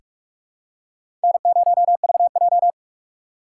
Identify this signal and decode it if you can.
Morse code, heard as a keyed tone, about 45 wpm, 700 Hz. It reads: N0VJ
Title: USA Call Signs - 6x Character Spacing - 45wpm